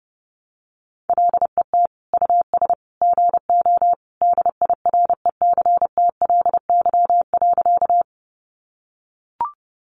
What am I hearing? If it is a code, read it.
Morse code, heard as a keyed tone, about 30 wpm, 700 Hz. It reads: LET US GO DIRECTLY.
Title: Sentences from Top 800 Words 30wpm